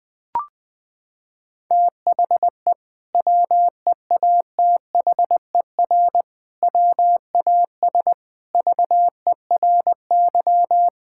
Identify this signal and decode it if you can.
Morse code, heard as a keyed tone, about 20 wpm, 700 Hz. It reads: THE WEATHER WAS VERY
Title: Sentences from Top 1000 Words - Review Part 5 - 20wpm